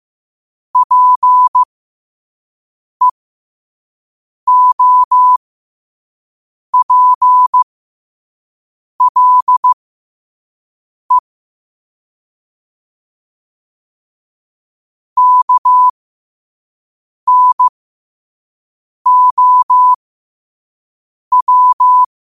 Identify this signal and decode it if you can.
Morse code, heard as a keyed tone, about 15 wpm, spaced as for 5 wpm, 1000 Hz. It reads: PEOPLE KNOW
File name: qotdfiles / 5wpm